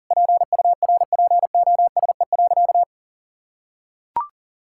Morse code, heard as a keyed tone, about 40 wpm, 700 Hz. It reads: PURPOSE.
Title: Sentences from Top 1600 Words 40wpm